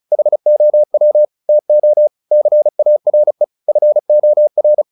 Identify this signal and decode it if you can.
Morse code, heard as a keyed tone, about 34 wpm, 600 Hz. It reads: HOW TO CARE FOR
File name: the old man and the sea 2 35wpm